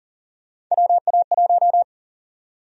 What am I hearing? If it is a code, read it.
Morse code, heard as a keyed tone, about 40 wpm, 700 Hz. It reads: WA1
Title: USA Call Sign Prefixes - 40wpm